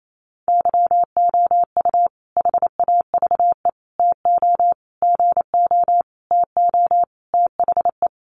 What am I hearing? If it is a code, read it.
Morse code, heard as a keyed tone, about 28 wpm, 700 Hz. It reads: YOU HAVE TO GO TO THE
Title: Sentences from Top 1300 Words 28wpm